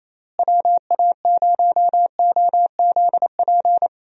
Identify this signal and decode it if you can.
Morse code, heard as a keyed tone, about 28 wpm, 700 Hz. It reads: WA0OZP
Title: USA Call Signs - 6 Characters - Repeated 2x - 28wpm